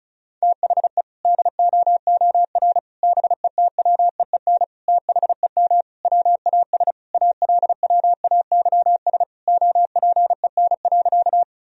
Morse code, heard as a keyed tone, about 36 wpm, 700 Hz. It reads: THE DOOR BETWEEN THEM WAS ALWAYS OPEN.
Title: Sentences from Top 1000 Words - Review Part 1 - 35wpm